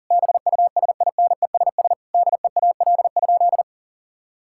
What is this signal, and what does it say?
Morse code, heard as a keyed tone, about 40 wpm, 700 Hz. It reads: BUSINESS DEAL?
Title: Sentences from Top 1000 Words - Review Part 6 - 40wpm